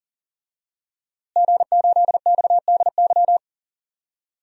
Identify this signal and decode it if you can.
Morse code, heard as a keyed tone, about 37 wpm, 700 Hz. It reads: G8XDY
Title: DXCC-100 Call Signs - 5 Characters - Repeated 3x - 40wpm